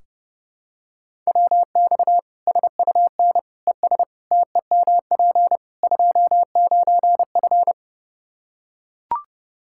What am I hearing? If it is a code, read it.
Morse code, heard as a keyed tone, about 30 wpm, 700 Hz. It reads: WX SUN ES TEMP 29F
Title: Practice QSO - Exchange 2 - Info - 30wpm